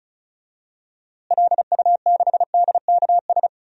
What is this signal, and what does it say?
Morse code, heard as a keyed tone, about 34 wpm, 700 Hz. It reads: LU6DKS
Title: DXCC-100 Call Signs - 6 Characters - Repeated 3x - 35wpm